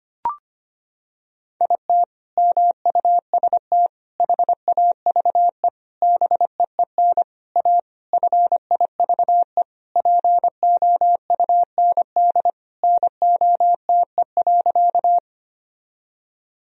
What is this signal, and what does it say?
Morse code, heard as a keyed tone, about 25 wpm, 700 Hz. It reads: IT MUST HAVE BEEN A FIVE POUND NOTE.
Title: Sentences from Top 1000 Words - Review Part 2 - 25wpm